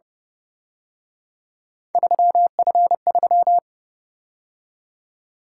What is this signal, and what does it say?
Morse code, heard as a keyed tone, about 29 wpm, 700 Hz. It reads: 3F3
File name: Speed-Racing - Three Letter-Number Combinations - 25wpm